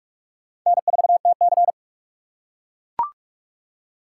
Morse code, heard as a keyed tone, about 44 wpm, 700 Hz. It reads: N4TC